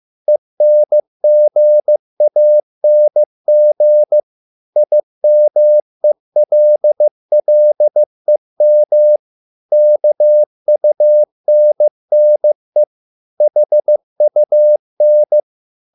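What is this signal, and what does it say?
Morse code, heard as a keyed tone, about 15 wpm, 600 Hz. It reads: ENGANG IMELLEM KUNNE HUN
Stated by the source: Morse code at a slow 15 wpm from CW med Gnister.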